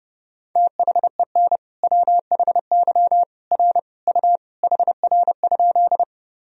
Morse code, heard as a keyed tone, about 30 wpm, 700 Hz. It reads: THEN WHY R U HR?
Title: Sentences from Top 200 Words and Abbreviations 30wpm